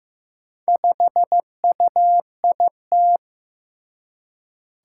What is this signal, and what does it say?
Morse code, heard as a keyed tone, about 15 wpm, 700 Hz. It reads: 5UIT
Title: Four Challenging Letter-Number Combinations 15wpm